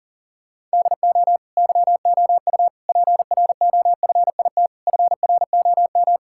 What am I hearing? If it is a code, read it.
Morse code, heard as a keyed tone, about 40 wpm, 700 Hz. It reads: DO YOU PROFIT FROM
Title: Sentences from Top 1000 Words - Review Part 5 - 40wpm